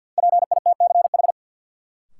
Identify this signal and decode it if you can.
Morse code, heard as a keyed tone, about 52 wpm, 700 Hz. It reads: PITCH
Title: Sentences from Top 1300 Words - New Word Drill Alt - 50wpm